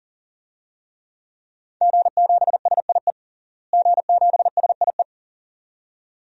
Morse code, heard as a keyed tone, about 40 wpm, 700 Hz. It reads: G7SIE G7SIE
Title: DXCC-100 Call Signs - 5 Characters - Repeated 2x - 40wpm